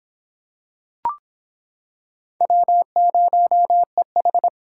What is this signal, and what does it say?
Morse code, heard as a keyed tone, about 25 wpm, 700 Hz. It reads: W0EH